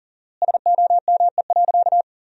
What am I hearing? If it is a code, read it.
Morse code, heard as a keyed tone, about 40 wpm, 700 Hz. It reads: SOME.
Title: 4x Word-Spacing - Sentences from Top 100 Words - 40wpm